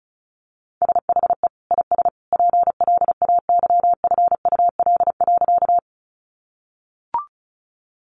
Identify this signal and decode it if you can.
Morse code, heard as a keyed tone, about 34 wpm, 700 Hz. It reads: SHE IS PLAYFUL.
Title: Easy Phrases 35wpm